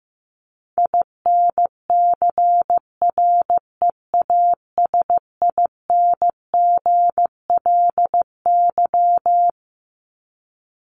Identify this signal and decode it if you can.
Morse code, heard as a keyed tone, about 15 wpm, 700 Hz. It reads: INCREASINGLY
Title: LY Suffix - Top 100 Words with Base Words 15wpm